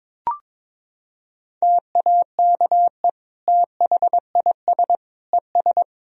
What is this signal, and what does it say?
Morse code, heard as a keyed tone, about 22 wpm, 700 Hz. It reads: TAKE THIS ES
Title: Sentences from Top 200 Words and Abbreviations 22wpm